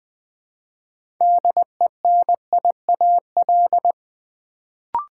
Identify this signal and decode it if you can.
Morse code, heard as a keyed tone, about 20 wpm, 700 Hz. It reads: DENIAL
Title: Build A Word - 2 Letters At A Time - 6-Letter Words - 20wpm